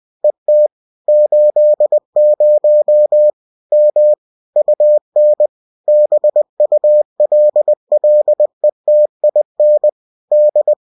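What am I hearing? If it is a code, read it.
Morse code, heard as a keyed tone, about 20 wpm, 600 Hz. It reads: ET 80 M UN BULLETIN D
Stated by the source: UFT bulletin sent in CW.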